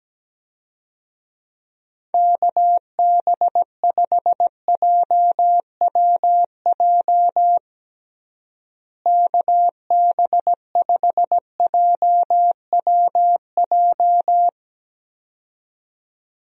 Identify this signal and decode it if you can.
Morse code, heard as a keyed tone, about 17 wpm, 700 Hz. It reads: KB5JWJ KB5JWJ
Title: USA Call Signs - 6 Characters - Repeated 2x - 17wpm